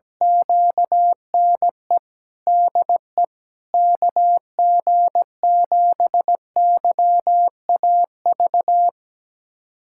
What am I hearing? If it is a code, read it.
Morse code, heard as a keyed tone, about 17 wpm, 700 Hz. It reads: QNE DE KG7YAV
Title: USA Callsign DE Callsign 17wpm